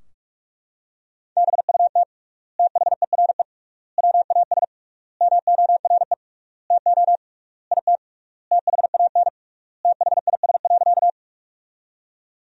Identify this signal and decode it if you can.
Morse code, heard as a keyed tone, about 41 wpm, 700 Hz. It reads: BUT THERE WAS MORE TO IT THAN THIS.
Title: 3x Word-Spacing - Sentences from Top 100 Words - 45wpm